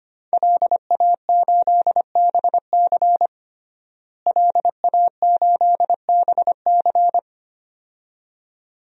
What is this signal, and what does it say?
Morse code, heard as a keyed tone, about 25 wpm, 700 Hz. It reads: LA8BC LA8BC
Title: DXCC-100 Call Signs - 5 Characters - Repeated 2x - 25wpm